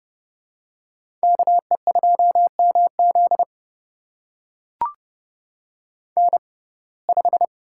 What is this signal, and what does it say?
Morse code, heard as a keyed tone, about 30 wpm, spaced as for 22 wpm, 700 Hz. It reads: KE2MZ N5
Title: USA Call Signs - 5 Characters - 6x Character Spacing - 30wpm